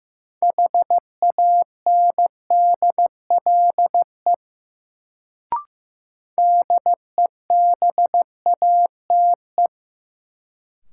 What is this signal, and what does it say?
Morse code, heard as a keyed tone, about 15 wpm, 700 Hz. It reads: HANDLE DEBATE
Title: Sentences from Top 1100 Words - New Word Drill Alt - 15wpm